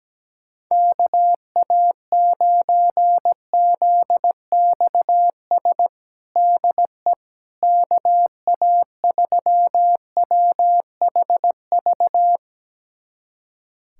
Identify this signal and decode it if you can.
Morse code, heard as a keyed tone, about 17 wpm, 700 Hz. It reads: KA9ZXS DE KA3WHV
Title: USA Callsign DE Callsign - 6 Characters - 17wpm